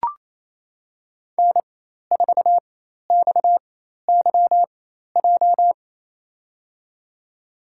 Morse code, heard as a keyed tone, about 28 wpm, spaced as for 13 wpm, 700 Hz. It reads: N4XYJ
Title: USA Call Signs - 4x Character Spacing - 28wpm